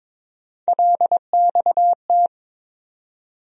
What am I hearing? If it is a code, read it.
Morse code, heard as a keyed tone, about 22 wpm, 700 Hz. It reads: LXT
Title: USA Call Sign Postfixes - 3 Letters 22wpm